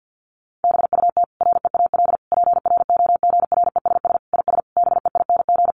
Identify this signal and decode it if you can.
Morse code, heard as a keyed tone, about 50 wpm, 700 Hz. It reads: BUT REAL PROGRESS IS BEING
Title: Sentences from Top 1400 Words 50wpm